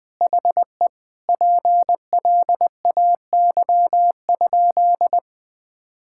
Morse code, heard as a keyed tone, about 20 wpm, 700 Hz. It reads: HE PLAY?
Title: Sentences from Top 1000 Words - Review Part 6 - 20wpm